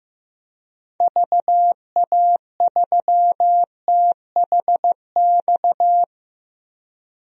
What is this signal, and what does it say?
Morse code, heard as a keyed tone, about 15 wpm, 700 Hz. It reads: VA3THX
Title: DXCC-100 Call Signs - 6 Characters - 15wpm